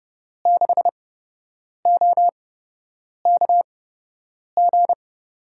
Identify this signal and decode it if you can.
Morse code, heard as a keyed tone, about 30 wpm, spaced as for 8 wpm, 700 Hz. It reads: 6OKG
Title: USA Call Signs - 5 Characters - 8x Character Spacing - 30wpm